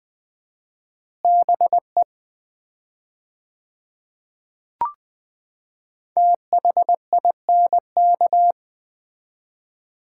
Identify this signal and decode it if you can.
Morse code, heard as a keyed tone, about 20 wpm, 700 Hz. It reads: BE THINK
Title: 4x Word-Spacing - Sets of 2 Words - 20wpm